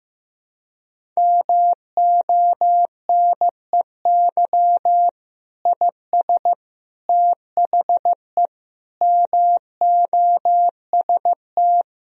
Morse code, heard as a keyed tone, about 15 wpm, 700 Hz. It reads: MONEY IS THE MOST
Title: Sentences from Top 1000 Words - Review Part 1 - 15wpm